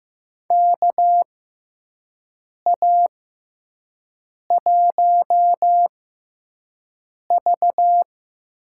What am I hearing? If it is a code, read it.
Morse code, heard as a keyed tone, about 15 wpm, spaced as for 5 wpm, 700 Hz. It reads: KA1V
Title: USA Call Signs - 6 Characters - 6x Character Spacing - 15wpm